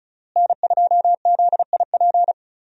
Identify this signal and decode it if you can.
Morse code, heard as a keyed tone, about 36 wpm, 700 Hz. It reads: N2ZIP
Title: USA Call Signs - 5 Characters - Repeated 3x - 35wpm